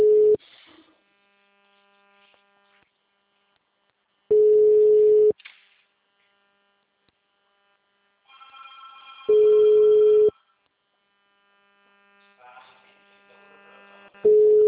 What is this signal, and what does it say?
Ringback tone (Europe, 425 Hz).
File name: out- -2039- - - .5129